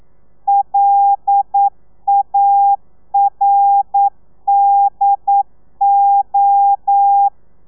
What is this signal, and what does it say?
Morse code, heard as a keyed tone, about 9 wpm, 800 Hz. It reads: LARDO
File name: telegrafia / i2tfj / Lez11 40